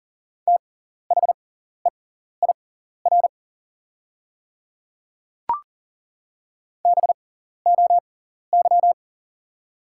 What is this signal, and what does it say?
Morse code, heard as a keyed tone, about 40 wpm, spaced as for 14 wpm, 700 Hz. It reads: THEIR BOY
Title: 6x Character Spacing - Top 200 Words - 40wpm